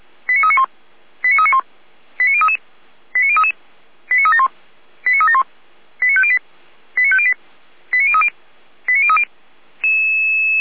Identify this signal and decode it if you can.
Five-tone selective call (ZVEI), heard as quick runs of five tones, tones of about 80 ms. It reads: 89391 89391 89033 89033 89371 89371 89508 89508 89030 89030